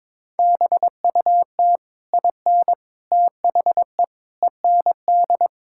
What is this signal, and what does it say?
Morse code, heard as a keyed tone, about 22 wpm, 700 Hz. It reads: BUT IN THE END
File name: practice / 22wpm / Sentences with 3-Word Phrases - 22wpm